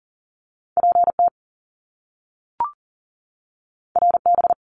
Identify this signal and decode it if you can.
Morse code, heard as a keyed tone, about 40 wpm, 700 Hz. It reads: PT RB
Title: Top Two Letter Combinations in Words 40wpm